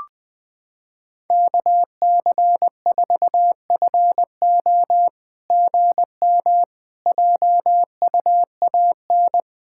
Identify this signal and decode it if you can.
Morse code, heard as a keyed tone, about 21 wpm, 700 Hz. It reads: KC4FO GM JUAN
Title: SOTA QSO Practice 20wpm